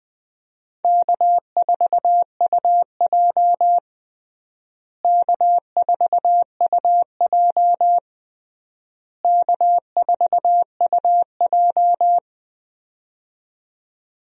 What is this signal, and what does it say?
Morse code, heard as a keyed tone, about 20 wpm, 700 Hz. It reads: K4UJ K4UJ K4UJ
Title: USA Call Signs - 4 Characters - Repeated 3x - 20wpm